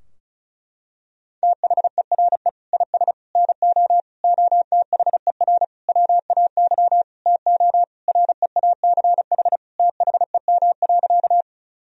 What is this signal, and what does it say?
Morse code, heard as a keyed tone, about 34 wpm, 700 Hz. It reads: THERE IS NO OTHER WAY TO REACH THEM.
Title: Sentences from Top 1000 Words - Review Part 2 - 35wpm